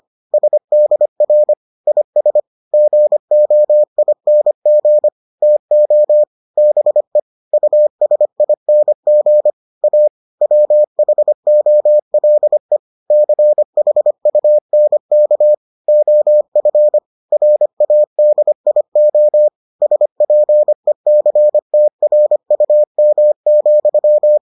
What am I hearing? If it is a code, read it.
Morse code, heard as a keyed tone, about 25 wpm, 600 Hz. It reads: SDR IS GOING TO BE USING A WHOLE CHUNK OF RADIO SPECTRUM,